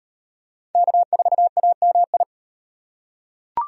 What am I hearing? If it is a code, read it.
Morse code, heard as a keyed tone, about 38 wpm, 700 Hz. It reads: K4AMI